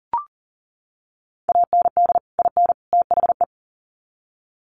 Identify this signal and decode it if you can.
Morse code, heard as a keyed tone, about 39 wpm, 700 Hz. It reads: AND IN THE
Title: Sets of 3 Words 40wpm